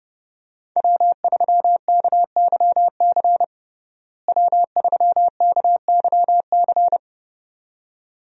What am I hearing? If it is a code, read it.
Morse code, heard as a keyed tone, about 32 wpm, 700 Hz. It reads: W3KYC W3KYC
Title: USA Call Signs - 5 Characters - Repeated 2x - 30wpm